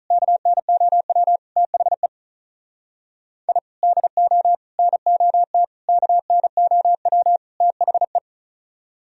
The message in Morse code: KNOW THE I DO NOT KNOW THE

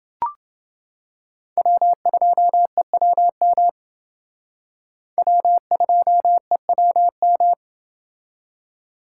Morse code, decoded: W2EWM W2EWM

